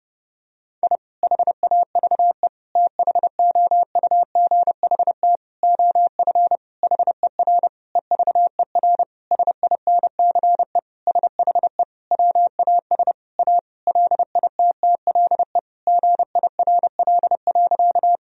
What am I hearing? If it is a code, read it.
Morse code, heard as a keyed tone, about 30 wpm, 700 Hz. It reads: I HAVE THOUGHT OF HER EVER SINCE SHE WAS A LITTLE GIRL.
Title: Sentences from Top 1000 Words - Review Part 1 - 30wpm